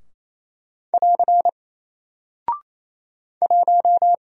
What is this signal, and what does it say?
Morse code, heard as a keyed tone, about 27 wpm, 700 Hz. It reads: <AR> 1
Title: Single Character (Letter:Number:Punctuation:Prosign) 28wpm